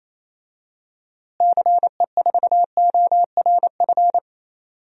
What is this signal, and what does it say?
Morse code, heard as a keyed tone, about 28 wpm, 700 Hz. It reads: CE4ORF
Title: DXCC-100 Call Signs - 6 Characters - Repeated 3x - 28wpm